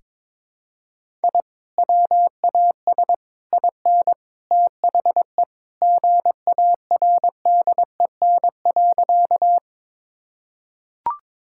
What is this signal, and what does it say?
Morse code, heard as a keyed tone, about 22 wpm, 700 Hz. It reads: I WAS IN THE GARDEN.
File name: practice / 22wpm / Sentences with 3-Word Phrases - 22wpm